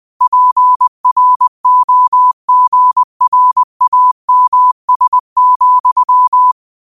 Morse code, heard as a keyed tone, about 20 wpm, 1000 Hz. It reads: PROGRAMS,
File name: qotdfiles / 20wpm